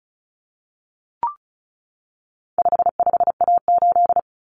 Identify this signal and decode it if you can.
Morse code, heard as a keyed tone, about 34 wpm, 700 Hz. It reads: 55A8
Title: Common Cut Numbers - 4-Digit 35wpm